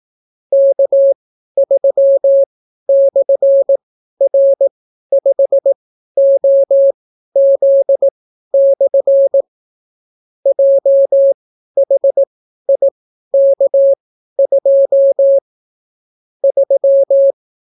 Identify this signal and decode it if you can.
Morse code, heard as a keyed tone, about 18 wpm, spaced as for 12 wpm, 550 Hz. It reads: K3/R5OZ/ JHIK2 3